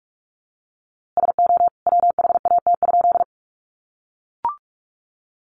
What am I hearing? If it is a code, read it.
Morse code, heard as a keyed tone, about 44 wpm, 700 Hz. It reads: SO WHAT?
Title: A Sound Of Thunder 45wpm